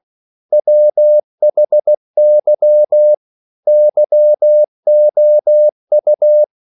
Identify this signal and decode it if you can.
Morse code, heard as a keyed tone, about 16 wpm, 600 Hz. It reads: WHY YOU
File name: CW / Practice / EnglishQuotations0416